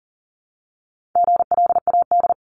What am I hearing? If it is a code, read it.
Morse code, heard as a keyed tone, about 38 wpm, 700 Hz. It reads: GLAD
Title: Multi-letter Phonemes Repeated 3x 40wpm